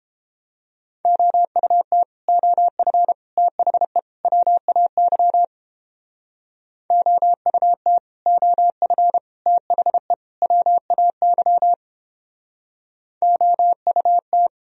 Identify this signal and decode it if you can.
Morse code, heard as a keyed tone, about 29 wpm, 700 Hz. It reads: OUT OF THE WAY OUT OF THE WAY OUT